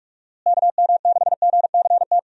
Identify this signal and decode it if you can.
Morse code, heard as a keyed tone, about 46 wpm, 700 Hz. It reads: KM6GCT
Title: USA Call Signs - 6 Characters - Repeated 2x - 45wpm